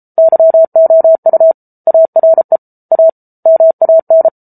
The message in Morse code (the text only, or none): YOU ARE A MAN